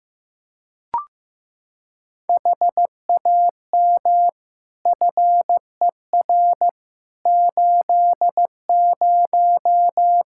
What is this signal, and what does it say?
Morse code, heard as a keyed tone, about 15 wpm, 700 Hz. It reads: HAM FER 80
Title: Practice QSO - Exchange 3 - More Info - 15wpm